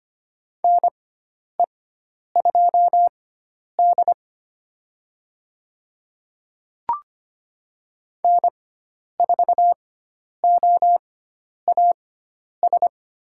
Morse code, heard as a keyed tone, about 25 wpm, spaced as for 10 wpm, 700 Hz. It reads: NE2D N4OAS
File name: Lesson 16g - Letter c - Call Signs 25wpm